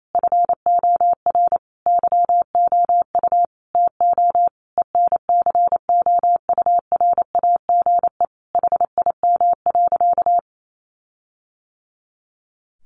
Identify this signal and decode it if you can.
Morse code, heard as a keyed tone, about 28 wpm, 700 Hz. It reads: FOR YOU TO ENCOURAGE HIM.
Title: Sentences from Top 1400 Words 28wpm